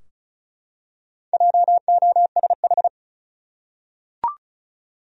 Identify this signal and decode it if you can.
Morse code, heard as a keyed tone, about 34 wpm, 700 Hz. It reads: JOSH
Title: USA Common Names - 35wpm